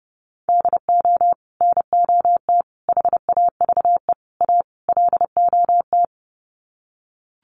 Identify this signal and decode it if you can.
Morse code, heard as a keyed tone, about 30 wpm, 700 Hz. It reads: DO NOT HAVE A LOT
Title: Sets of 5 Words Encore 30wpm